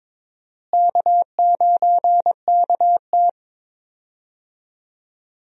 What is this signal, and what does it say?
Morse code, heard as a keyed tone, about 22 wpm, 700 Hz. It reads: K9KT
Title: USA Call Signs - 4 Characters - Repeated 2x - 22wpm